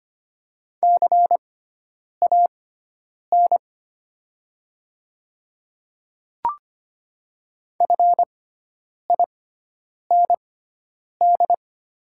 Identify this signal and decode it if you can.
Morse code, heard as a keyed tone, about 25 wpm, spaced as for 9 wpm, 700 Hz. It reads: CAN FIND
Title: 6x Character Spacing - Top 200 Words - 25wpm